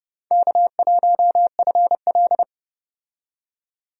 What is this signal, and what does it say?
Morse code, heard as a keyed tone, about 31 wpm, 700 Hz. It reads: K1FL
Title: USA Call Signs - 4 Characters - 30wpm